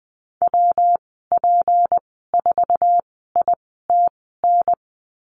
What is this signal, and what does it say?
Morse code, heard as a keyed tone, about 20 wpm, spaced as for 15 wpm, 700 Hz. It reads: WP4ITN
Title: 161 Lesson 26g - Letter g - Call Signs 20wpm